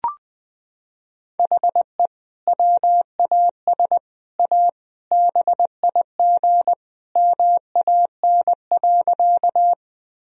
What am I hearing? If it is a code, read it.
Morse code, heard as a keyed tone, about 20 wpm, 700 Hz. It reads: HE WAS A BIG MAN.
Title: Sentences from Top 1000 Words - Review Part 1 - 20wpm